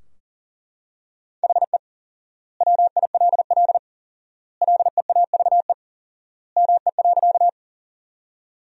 Morse code, heard as a keyed tone, about 40 wpm, 700 Hz. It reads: HE WILL LEAVE ME.
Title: 4x Word-Spacing - Sentences from Top 400 Words - 40wpm